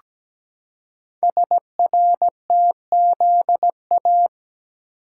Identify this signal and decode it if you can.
Morse code, heard as a keyed tone, about 17 wpm, 700 Hz. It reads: SRTZA
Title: Code Groups - 5 Letters - 17wpm